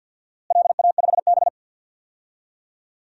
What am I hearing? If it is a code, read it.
Morse code, heard as a keyed tone, about 51 wpm, 700 Hz. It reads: LA5B
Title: DXCC-100 Call Signs - 4 Characters - Repeated 2x - 50wpm